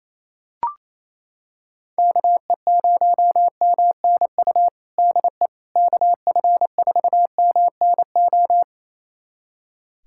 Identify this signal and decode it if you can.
Morse code, heard as a keyed tone, about 28 wpm, 700 Hz. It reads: KE0MNU DE KF4MNO
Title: USA Callsign DE Callsign - 6 Characters - 28wpm